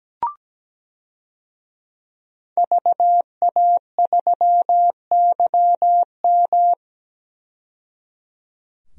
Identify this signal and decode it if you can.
Morse code, heard as a keyed tone, about 17 wpm, 700 Hz. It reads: VA3YM
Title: DXCC-100 Call Signs - 5 Characters - 17wpm